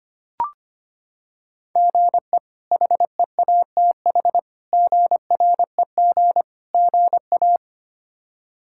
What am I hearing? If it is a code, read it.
Morse code, heard as a keyed tone, about 25 wpm, 700 Hz. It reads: GE HEATH GREG GA
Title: K1USN SST QSO Practice - 25wpm